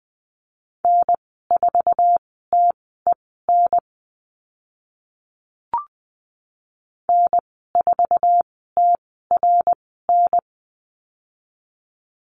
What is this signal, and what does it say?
Morse code, heard as a keyed tone, about 20 wpm, spaced as for 14 wpm, 700 Hz. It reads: N4TEN N4TRN
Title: 062 Lesson 11g - Letter h - Call Signs 20wpm